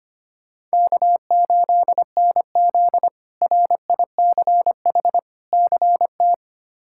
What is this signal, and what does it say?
Morse code, heard as a keyed tone, about 25 wpm, 700 Hz. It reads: K8NZ RICH CT
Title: K1USN SST QSO Practice - 25wpm